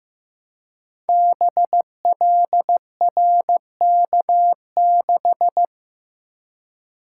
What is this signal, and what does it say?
Morse code, heard as a keyed tone, about 15 wpm, 700 Hz. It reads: BLRK6
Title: Code Groups - 5 Letter:Numbers - 15wpm